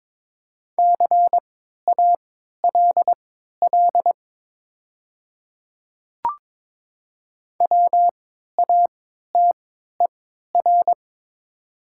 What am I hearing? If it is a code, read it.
Morse code, heard as a keyed tone, about 22 wpm, spaced as for 13 wpm, 700 Hz. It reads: CALL WATER
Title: 3x Character Spacing - Top 100 Words - 22wpm